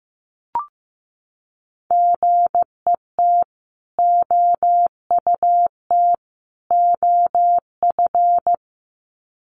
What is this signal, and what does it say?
Morse code, heard as a keyed tone, about 15 wpm, 700 Hz. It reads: GET OUT OF
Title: Sets of 3 Words 15wpm